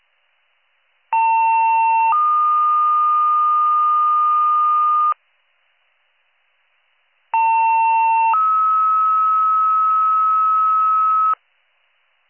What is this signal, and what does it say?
Two-tone page, heard as a one-second tone, then a three-second tone, about 900 then 1220 Hz, about 900 then 1290 Hz. It is heard from 1.1 to 11.3 s.